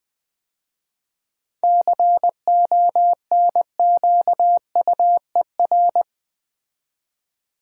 Morse code, heard as a keyed tone, about 20 wpm, 700 Hz. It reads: CONQUER
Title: 7 Letter Words - First 3 Letters Given - 20wpm